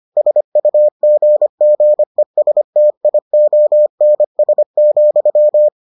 Morse code, heard as a keyed tone, about 25 wpm, 600 Hz. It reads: SUGGESTIONS,